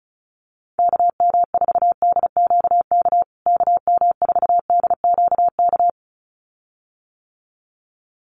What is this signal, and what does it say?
Morse code, heard as a keyed tone, about 34 wpm, 700 Hz. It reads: KM4DQK KM4DQK
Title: USA Call Signs Repeated 2x 35wpm